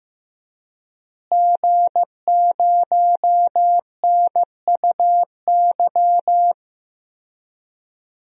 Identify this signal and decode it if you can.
Morse code, heard as a keyed tone, about 15 wpm, 700 Hz. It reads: G0NUY
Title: DXCC-100 Call Signs - 5 Characters - Repeated 2x - 15wpm